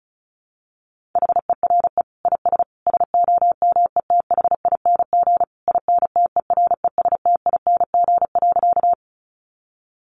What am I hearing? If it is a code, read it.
Morse code, heard as a keyed tone, about 34 wpm, 700 Hz. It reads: HERE IS SOMETHING INTERESTING.
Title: Sentences from Top 1000 Words 35wpm